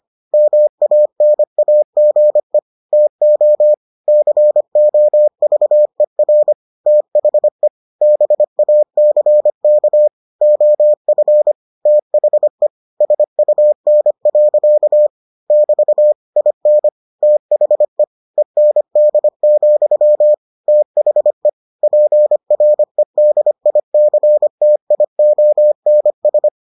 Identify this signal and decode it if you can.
Morse code, heard as a keyed tone, about 25 wpm, 600 Hz. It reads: MANAGE TO COVER THE BACK OF THE SUN. <BT> IN THE END, THE PREDICTIONS